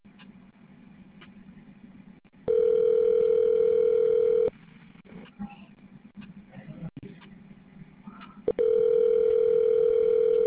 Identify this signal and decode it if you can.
Ringback tone (North America), 440 + 480 Hz.